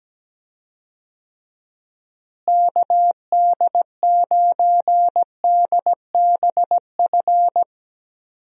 Morse code, heard as a keyed tone, about 17 wpm, 700 Hz. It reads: KD9DBF
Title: USA Call Signs - 6 Characters - Repeated 2x - 17wpm